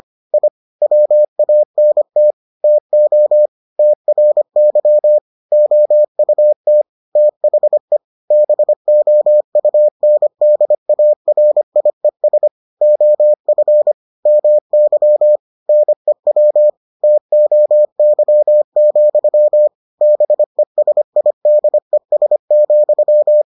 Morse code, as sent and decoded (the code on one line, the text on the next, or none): .. / .-- .- -. - / - --- / - .-. -.-- / --- ..- - / - .... / -... --- ..- -. -.. .- .-. .. ... / --- ..-. / -- -.-- / -. .-- / - --- -.-- --..-- / -... ... .. -.. ... --..--
I WANT TO TRY OUT THE BOUNDARIES OF MY NEW TOY, BESIDES,